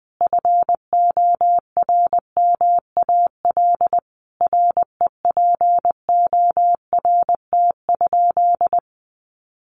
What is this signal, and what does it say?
Morse code, heard as a keyed tone, about 20 wpm, 700 Hz. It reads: FORMAL REPORT?
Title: Sentences from Top 1300 Words 20wpm